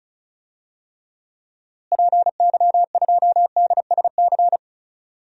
Morse code, heard as a keyed tone, about 36 wpm, 700 Hz. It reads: PY2DSC
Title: DXCC-100 Call Signs - 6 Characters - Repeated 2x - 35wpm